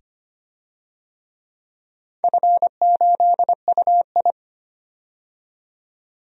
Morse code, heard as a keyed tone, about 25 wpm, 700 Hz. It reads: F8UI
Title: DXCC-100 Call Signs - 4 Characters - 25wpm